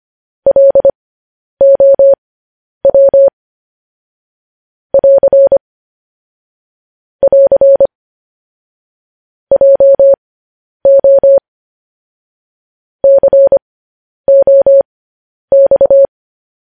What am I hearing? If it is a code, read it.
Morse code, heard as a keyed tone, about 25 wpm, spaced as for 10 wpm, 550 Hz. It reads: LOW <AR> <AR> JO COX
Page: news headlines in morse code at 10 wpm